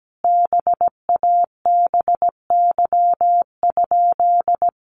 Morse code, heard as a keyed tone, about 17 wpm, 700 Hz. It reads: BABY?
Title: Sentences from Top 1300 Words 17wpm